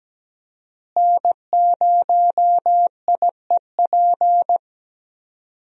Morse code, heard as a keyed tone, about 17 wpm, 700 Hz. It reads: N0IEP
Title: USA Call Signs - 5 Characters - Repeated 3x - 17wpm